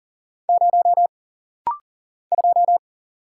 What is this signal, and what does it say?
Morse code, heard as a keyed tone, about 42 wpm, 700 Hz. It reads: 0 2
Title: Single Letter-Number Alt - Rapid-Fire 40wpm